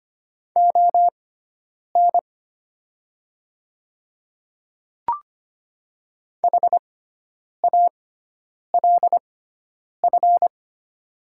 Morse code, heard as a keyed tone, about 25 wpm, spaced as for 9 wpm, 700 Hz. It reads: ON HALF
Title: 6x Character Spacing - Top 500 Words - 25wpm